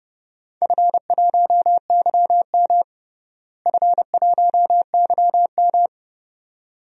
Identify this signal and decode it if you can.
Morse code, heard as a keyed tone, about 30 wpm, 700 Hz. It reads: F1YM F1YM
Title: DXCC-100 Call Signs - 4 Characters - Repeated 2x - 30wpm